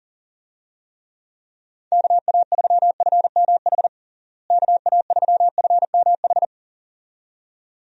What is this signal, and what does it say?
Morse code, heard as a keyed tone, about 40 wpm, 700 Hz. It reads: KA3FMH KA3FMH
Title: USA Call Signs - 6 Characters - Repeated 2x - 40wpm